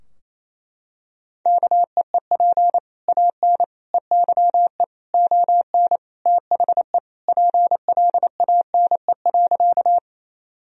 Morse code, heard as a keyed tone, about 28 wpm, 700 Hz. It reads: KEEP AN EYE ON THE PLANE.
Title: Sentences from Top 1000 Words - Review Part 3 - 28wpm